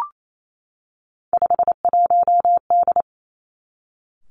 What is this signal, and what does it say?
Morse code, heard as a keyed tone, about 28 wpm, 700 Hz. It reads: 51D
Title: Three Challenging Letter-Number Combinations 28wpm